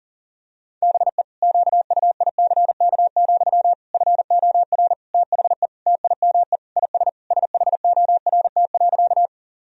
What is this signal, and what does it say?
Morse code, heard as a keyed tone, about 40 wpm, 700 Hz. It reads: BE QUICK, FOR THE TIME IS SHORT.
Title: Sentences from Top 1000 Words - Review Part 2 - 40wpm